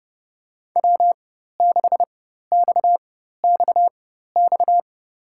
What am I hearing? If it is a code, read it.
Morse code, heard as a keyed tone, about 32 wpm, spaced as for 14 wpm, 700 Hz. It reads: W6XXX
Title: USA Call Signs - 5 Characters - 4x Character Spacing - 30wpm